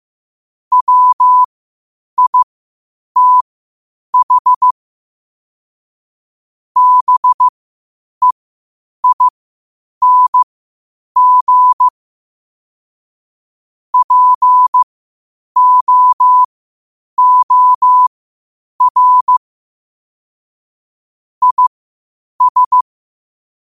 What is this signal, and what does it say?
Morse code, heard as a keyed tone, about 15 wpm, spaced as for 8 wpm, 1000 Hz. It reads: WITH BEING POOR IS
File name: qotdfiles / 8wpm